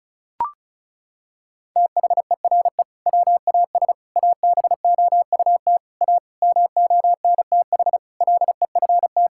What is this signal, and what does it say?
Morse code, heard as a keyed tone, about 36 wpm, 700 Hz. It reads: THERE WAS ABOUT A MONTH LEFT
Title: Sentences from Top 1000 Words - Review Part 5 - 35wpm